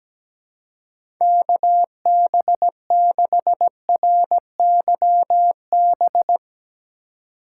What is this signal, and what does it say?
Morse code, heard as a keyed tone, about 17 wpm, 700 Hz. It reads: KB6RYB
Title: USA Call Signs - 2x Character Spacing - 17wpm